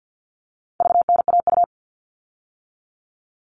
Morse code, heard as a keyed tone, about 50 wpm, 700 Hz. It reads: VNAU
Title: All Cut Numbers - 4-Digit 50wpm